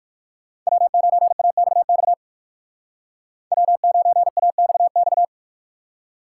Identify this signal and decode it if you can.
Morse code, heard as a keyed tone, about 57 wpm, 700 Hz. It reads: W9AXX W9AXX